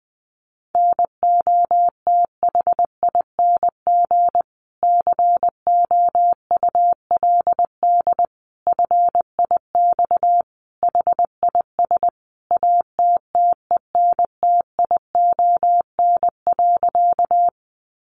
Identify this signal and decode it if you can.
Morse code, heard as a keyed tone, about 20 wpm, 700 Hz. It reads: NOTHING COULD FIX HIS ATTENTION.
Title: Sentences from Top 1400 Words 20wpm